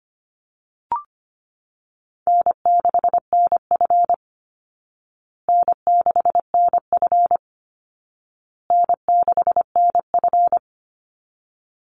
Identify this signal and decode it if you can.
Morse code, heard as a keyed tone, about 25 wpm, 700 Hz. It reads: N6NF N6NF N6NF
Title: Easy Call Signs Repeated 3x - 25wpm